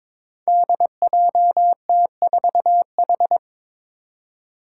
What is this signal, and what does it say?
Morse code, heard as a keyed tone, about 21 wpm, 700 Hz. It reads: DJT4H
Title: Five Challenging Letter-Number Combinations 22wpm